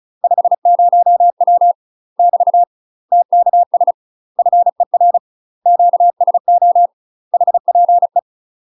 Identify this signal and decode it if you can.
Morse code, heard as a keyed tone, about 34 wpm, 700 Hz. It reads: 50W <BT> TKS FER QSO HPE
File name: UK MASTER QSOs TRAINING / 35WPM / UK MASTER QSOs 0042